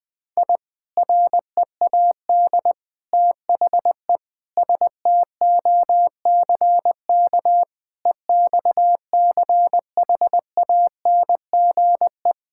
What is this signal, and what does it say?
Morse code, heard as a keyed tone, about 20 wpm, 700 Hz. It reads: I READ THE STOCK EXCHANGE
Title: Sentences from Top 1000 Words - Review Part 6 - 20wpm